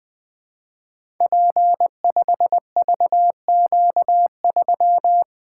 Morse code, heard as a keyed tone, about 20 wpm, 700 Hz. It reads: P5VQ3
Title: Code Groups - 5 Letter:Numbers - 20wpm